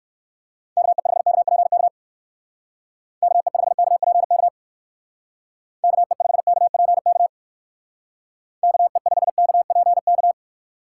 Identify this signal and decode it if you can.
Morse code, heard as a keyed tone, about 59 wpm, 700 Hz. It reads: KE5KPK KE5KPK KE5KPK KE5KPK